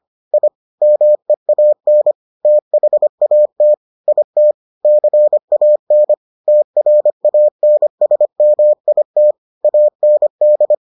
Morse code, heard as a keyed tone, about 25 wpm, 600 Hz. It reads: I MEAN THAT IT CAN TRANSMIT AND